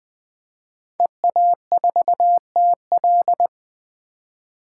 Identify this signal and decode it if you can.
Morse code, heard as a keyed tone, about 20 wpm, 700 Hz. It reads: EA4TL
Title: 2023 POTA Hunter Call Signs - 20wpm